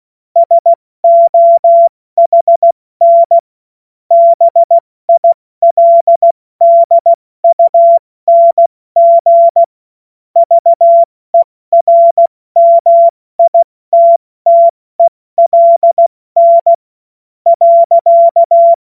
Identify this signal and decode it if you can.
Morse code, heard as a keyed tone, about 16 wpm, spaced as for 14 wpm, 680 Hz. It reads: SOHN BILDUNG VERMITTELN.
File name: Morsekurs / Witze / Witz 09 Januar 2024 70 F80